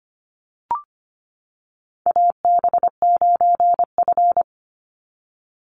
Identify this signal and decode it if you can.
Morse code, heard as a keyed tone, about 25 wpm, 690 Hz. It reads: AB9F
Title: USA Call Signs 25wpm